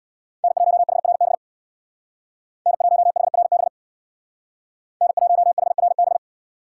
Morse code, heard as a keyed tone, about 64 wpm, 700 Hz. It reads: N1HRB N1HRB N1HRB